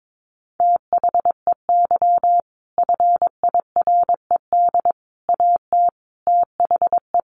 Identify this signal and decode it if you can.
Morse code, heard as a keyed tone, about 22 wpm, 700 Hz. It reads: THEY FIRED AT THE
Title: A Sound Of Thunder 22wpm